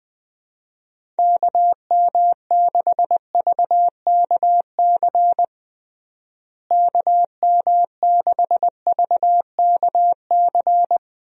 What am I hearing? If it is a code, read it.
Morse code, heard as a keyed tone, about 20 wpm, 700 Hz. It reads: KM6VKC KM6VKC
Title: USA Call Signs - 6 Characters - Repeated 2x - 20wpm